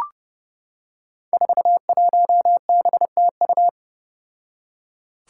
Morse code, heard as a keyed tone, about 30 wpm, 700 Hz. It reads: 41BTU
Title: Five Challenging Letter-Number Combinations 30wpm